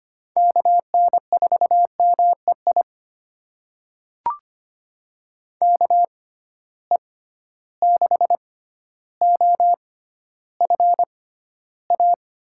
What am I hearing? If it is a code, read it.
Morse code, heard as a keyed tone, about 25 wpm, spaced as for 13 wpm, 700 Hz. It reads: KN4MEI KE6OFA
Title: USA Call Signs - 6 Characters - 6x Character Spacing - 25wpm